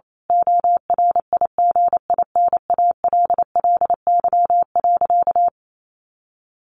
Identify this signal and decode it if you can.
Morse code, heard as a keyed tone, about 28 wpm, 700 Hz. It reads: ORIGINALLY.
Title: A Sound Of Thunder 28wpm